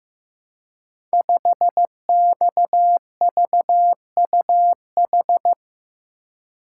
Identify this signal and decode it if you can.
Morse code, heard as a keyed tone, about 15 wpm, 700 Hz. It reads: 5XVUH
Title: Code Groups - 5 Letter:Numbers - 15wpm